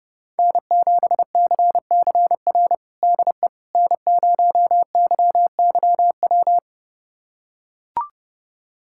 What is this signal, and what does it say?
Morse code, heard as a keyed tone, about 32 wpm, 700 Hz. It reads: N7CCR DE N0YYW
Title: USA Callsign DE Callsign - 5 Characters - 30wpm